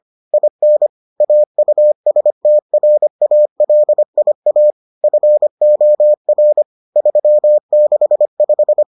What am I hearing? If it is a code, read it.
Morse code, heard as a keyed tone, about 25 wpm, 600 Hz. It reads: IN AUSTRALIA FOR 365